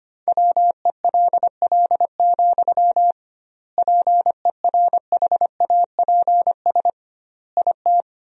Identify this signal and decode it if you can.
Morse code, heard as a keyed tone, about 25 wpm, 700 Hz. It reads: WELL, PERHAPS IT
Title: 2x Word-Spacing - Sentences from Top 500 Words - 25wpm